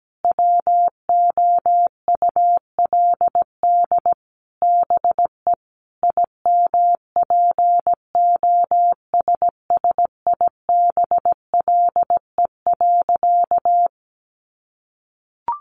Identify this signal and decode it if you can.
Morse code, heard as a keyed tone, about 17 wpm, 700 Hz. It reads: WOULD BE IMPOSSIBLE.
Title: Sentences from Top 1100 Words 17wpm